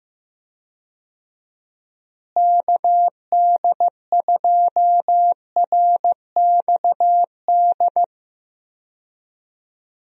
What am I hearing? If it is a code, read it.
Morse code, heard as a keyed tone, about 15 wpm, 700 Hz. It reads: KD2RXD
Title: USA Call Signs - 6 Characters - Repeated 2x - 15wpm